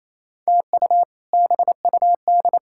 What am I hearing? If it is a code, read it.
Morse code, heard as a keyed tone, about 27 wpm, 700 Hz. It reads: TU BUD
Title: K1USN SST QSO Practice - 28wpm